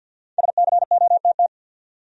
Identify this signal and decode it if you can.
Morse code, heard as a keyed tone, about 53 wpm, 700 Hz. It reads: SCOTT
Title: USA Common Names - Max 5 Letters - 50wpm